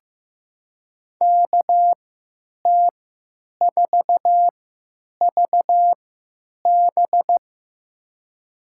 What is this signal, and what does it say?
Morse code, heard as a keyed tone, about 15 wpm, spaced as for 9 wpm, 700 Hz. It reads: KT4VB
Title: USA Call Signs - 5 Characters - 3x Character Spacing - 15wpm